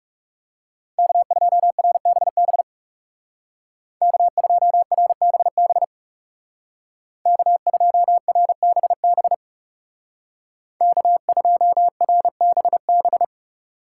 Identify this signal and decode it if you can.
Morse code, heard as a keyed tone, about 36 wpm, 700 Hz. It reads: K2RBB K2RBB K2RBB K2RBB
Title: Speed Racing - USA 1x3 Call Signs 30wpm